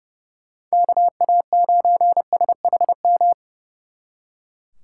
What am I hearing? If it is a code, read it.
Morse code, heard as a keyed tone, about 36 wpm, 700 Hz. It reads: KA9SHM
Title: Speed Racing - USA 2x3 Call Signs 30wpm